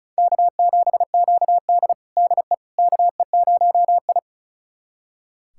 Morse code, heard as a keyed tone, about 36 wpm, 700 Hz. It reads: K7QD DE KE0I
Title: USA Callsign DE Callsign - 4 Characters - 35wpm